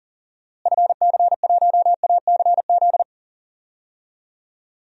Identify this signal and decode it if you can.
Morse code, heard as a keyed tone, about 40 wpm, 700 Hz. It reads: FC1ACZ
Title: DXCC-100 Call Signs - 6 Characters - Repeated 2x - 40wpm